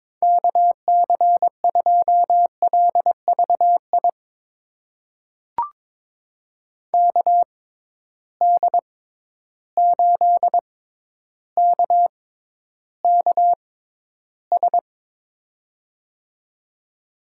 Morse code, heard as a keyed tone, about 22 wpm, spaced as for 11 wpm, 700 Hz. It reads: KC2LVI KD8KKS
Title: USA Call Signs - 6x Character Spacing - 22wpm